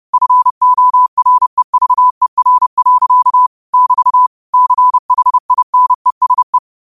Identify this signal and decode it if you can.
Morse code, heard as a keyed tone, about 30 wpm, 1000 Hz. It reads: FOREVER. <BT> CHINESE
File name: qotdfiles / 30wpm